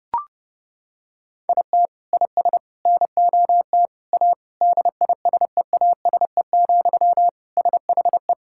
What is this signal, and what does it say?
Morse code, heard as a keyed tone, about 30 wpm, 700 Hz. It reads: IT IS NOT A DISEASE, SHE
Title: Sentences from Top 1000 Words - Review Part 6 - 30wpm